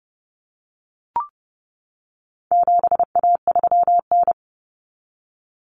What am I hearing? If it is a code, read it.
Morse code, heard as a keyed tone, about 30 wpm, 700 Hz. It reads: 7A3N
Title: Common Cut Numbers - 4-Digit 30wpm